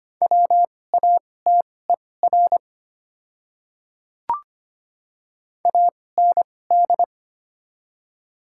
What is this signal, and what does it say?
Morse code, heard as a keyed tone, about 25 wpm, spaced as for 18 wpm, 700 Hz. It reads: WATER AND
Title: 2x Character Spacing - Top 200 Words - 25wpm